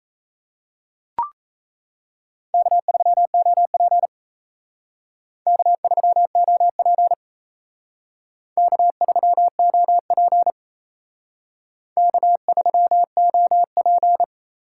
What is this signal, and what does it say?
Morse code, heard as a keyed tone, about 35 wpm, 700 Hz. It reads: K3OP K3OP K3OP K3OP
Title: Speed Racing - USA Call Signs 28wpm